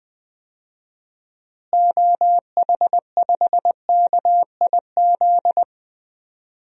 Morse code, heard as a keyed tone, about 21 wpm, 700 Hz. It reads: OH5KIZ
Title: DXCC-100 Call Signs - 6 Characters - Repeated 3x - 20wpm